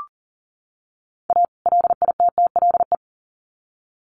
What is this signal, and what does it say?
Morse code, heard as a keyed tone, about 40 wpm, 700 Hz. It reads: A LITTLE
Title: Sets of 2 Words 40wpm